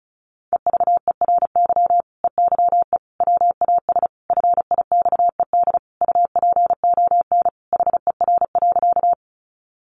Morse code, heard as a keyed tone, about 34 wpm, 700 Hz. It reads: EVERY EYE WAS FIXED UPON HER.
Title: Sentences from Top 1100 Words 35wpm